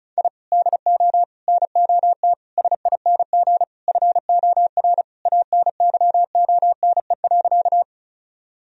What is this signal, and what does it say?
Morse code, heard as a keyed tone, about 34 wpm, 700 Hz. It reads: I DO NOT SING FOR ANYONE.
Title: Sentences from Top 1000 Words - Review Part 2 - 35wpm